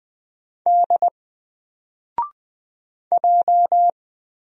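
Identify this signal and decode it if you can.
Morse code, heard as a keyed tone, about 20 wpm, 700 Hz. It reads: D J
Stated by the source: Single letter and number Morse drill at 20 wpm, from N4LCD.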